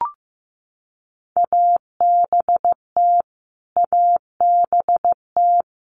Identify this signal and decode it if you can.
Morse code, heard as a keyed tone, about 15 wpm, 700 Hz. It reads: ABT ABT
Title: Abbreviations Repeated 3x - 15wpm